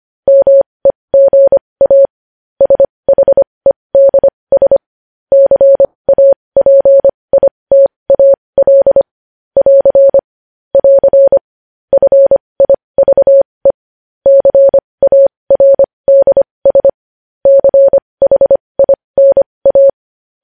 Morse code, heard as a keyed tone, about 25 wpm, spaced as for 20 wpm, 550 Hz. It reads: MEGA SHEDS CAPITAL <AR> <AR> FIVE CARDS CHINA
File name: podcast / News headlines-1x-25x20